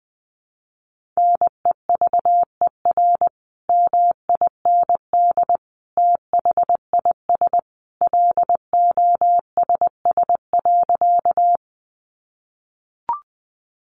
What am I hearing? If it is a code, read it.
Morse code, heard as a keyed tone, about 20 wpm, 700 Hz. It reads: NEVER MIND THIS LOSS.
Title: Sentences from Top 1400 Words 20wpm